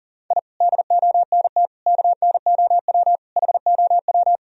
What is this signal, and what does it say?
Morse code, heard as a keyed tone, about 40 wpm, 700 Hz. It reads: I DONT KNOW HOW
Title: Sentences from Top 1000 Words - Review Part 2 - 40wpm